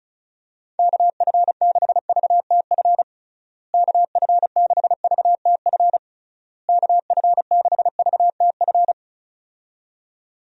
Morse code, heard as a keyed tone, about 34 wpm, 700 Hz. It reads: KF6VTF KF6VTF KF6VTF
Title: USA Call Signs - 6 Characters - Repeated 3x - 35wpm